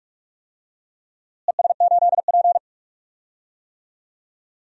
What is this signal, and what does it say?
Morse code, heard as a keyed tone, about 42 wpm, 700 Hz. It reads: ES8P
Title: DXCC-100 Call Signs - 4 Characters - Repeated 2x - 45wpm